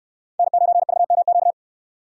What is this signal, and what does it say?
Morse code, heard as a keyed tone, about 67 wpm, 700 Hz. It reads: N9VMQ